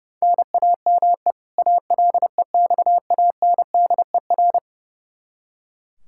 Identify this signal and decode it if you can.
Morse code, heard as a keyed tone, about 30 wpm, 700 Hz. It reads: NAME ALEXANDER
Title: Practice QSO - Exchange 1 - Beginning - 30wpm